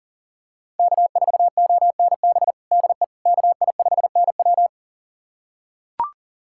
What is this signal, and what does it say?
Morse code, heard as a keyed tone, about 37 wpm, 700 Hz. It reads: K4ONB DE KI5NW
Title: USA Callsign DE Callsign 40wpm